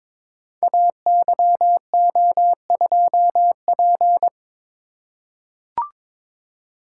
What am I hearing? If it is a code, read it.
Morse code, heard as a keyed tone, about 22 wpm, 690 Hz. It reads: AYO2P
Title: Code Groups - 5 Letter:Numbers - 22wpm